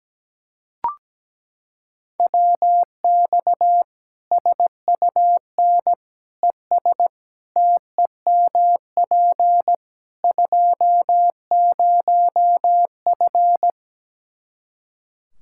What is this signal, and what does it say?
Morse code, heard as a keyed tone, about 17 wpm, 700 Hz. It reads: WX SUN ES TEMP 20F
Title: Practice QSO - Exchange 2 - Info - 17wpm